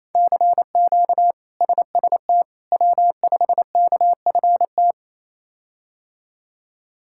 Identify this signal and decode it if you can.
Morse code, heard as a keyed tone, about 27 wpm, 700 Hz. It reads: CQ SST W5KFT
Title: K1USN SST QSO Practice - 28wpm